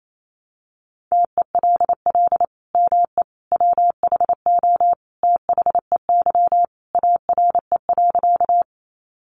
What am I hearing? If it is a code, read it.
Morse code, heard as a keyed tone, about 28 wpm, 700 Hz. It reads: TELL ME WHO THEY ARE.
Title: Sentences from Top 700 Words 28wpm